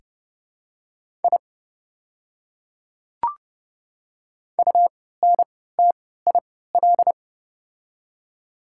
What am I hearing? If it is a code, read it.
Morse code, heard as a keyed tone, about 31 wpm, spaced as for 17 wpm, 690 Hz. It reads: I UNTIL
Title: 3x Character Spacing - Top 300 Words - 30wpm